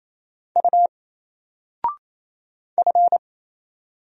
Morse code, heard as a keyed tone, about 27 wpm, 700 Hz. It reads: U F
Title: Single Letter-Number Alt 28wpm